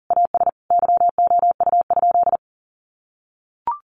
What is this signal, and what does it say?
Morse code, heard as a keyed tone, about 40 wpm, 700 Hz. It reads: AS YOU?
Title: Sentences from Top 1000 Words 40wpm